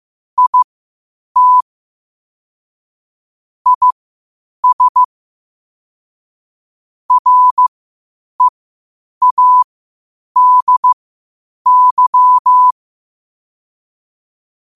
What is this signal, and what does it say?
Morse code, heard as a keyed tone, about 15 wpm, spaced as for 8 wpm, 1000 Hz. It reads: IT IS READY
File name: qotdfiles / 8wpm